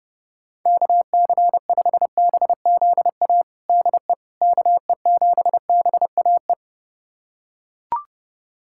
Morse code, heard as a keyed tone, about 30 wpm, 700 Hz. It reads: KC5BZA DE KE7BAE
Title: USA Callsign DE Callsign 30wpm